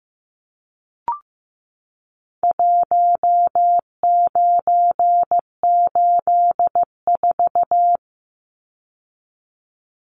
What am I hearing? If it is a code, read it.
Morse code, heard as a keyed tone, about 15 wpm, 700 Hz. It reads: 1984
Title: Numbers - Contest Style - 4-Digit 15wpm